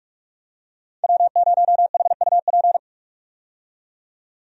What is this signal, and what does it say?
Morse code, heard as a keyed tone, about 43 wpm, 700 Hz. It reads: W0HUP
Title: USA Call Signs - 5 Characters - 45wpm